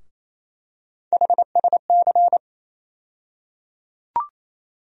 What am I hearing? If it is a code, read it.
Morse code, heard as a keyed tone, about 27 wpm, 700 Hz. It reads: HSC
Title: USA Call Sign Postfixes - 3 Letters 28wpm